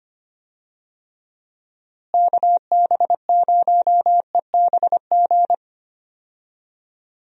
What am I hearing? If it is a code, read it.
Morse code, heard as a keyed tone, about 25 wpm, 700 Hz. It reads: KB0EBG
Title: USA Call Signs - 6 Characters - Repeated 2x - 25wpm